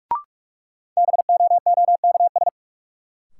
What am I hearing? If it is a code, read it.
Morse code, heard as a keyed tone, about 42 wpm, 700 Hz. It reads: BOOKS
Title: Sentences from Top 1200 Words - New Word Drill Alt - 45wpm